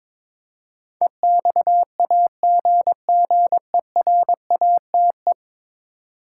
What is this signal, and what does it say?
Morse code, heard as a keyed tone, about 22 wpm, 700 Hz. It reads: EXAGGERATE
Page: morse-code-ninja-practice-22wpm